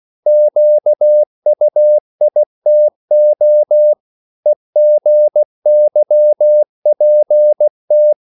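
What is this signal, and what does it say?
Morse code, heard as a keyed tone, about 16 wpm, 600 Hz. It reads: QUITO EGYPT
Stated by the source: CW practice of national capitals at 16 words per minute, from ON6ZQ.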